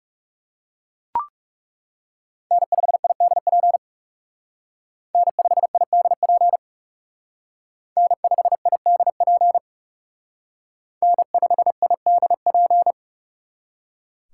Morse code, heard as a keyed tone, about 38 wpm, 700 Hz. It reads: N5IDP N5IDP N5IDP N5IDP